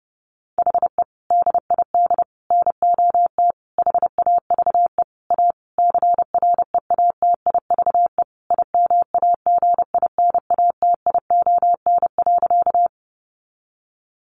Morse code, heard as a keyed tone, about 28 wpm, 700 Hz. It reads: HE DID NOT HAVE A CREATIVE IMAGINATION.
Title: Sentences from Top 1300 Words 30wpm